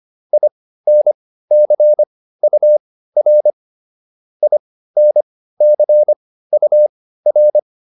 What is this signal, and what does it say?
Morse code, heard as a keyed tone, about 25 wpm, spaced as for 15 wpm, 600 Hz. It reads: INCUR INCUR
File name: IN5 15